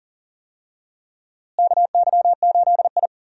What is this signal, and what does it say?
Morse code, heard as a keyed tone, about 40 wpm, 700 Hz. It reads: KY8I
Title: USA Call Signs - 4 Characters - Repeated 3x - 40wpm